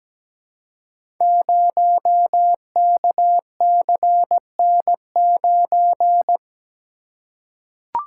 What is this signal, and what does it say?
Morse code, heard as a keyed tone, about 17 wpm, 700 Hz. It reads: 0KCN9
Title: Code Groups - 5 Letter:Numbers - 17wpm